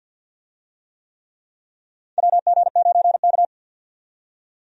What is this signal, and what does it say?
Morse code, heard as a keyed tone, about 49 wpm, 700 Hz. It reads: WG9K
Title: USA Call Signs - 4 Characters - 50wpm